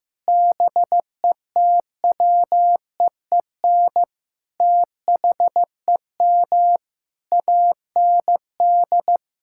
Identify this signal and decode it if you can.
Morse code, heard as a keyed tone, about 15 wpm, 700 Hz. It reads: BETWEEN THEM AND
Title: Sentences from Top 1600 Words 15wpm